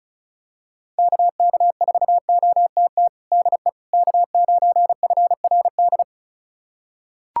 Morse code, heard as a keyed tone, about 36 wpm, 700 Hz. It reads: KK4OTT DE K9FRD
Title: USA Callsign DE Callsign 35wpm